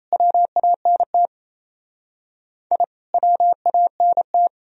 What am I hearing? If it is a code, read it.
Morse code, heard as a keyed tone, about 37 wpm, 700 Hz. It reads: WANT I WANT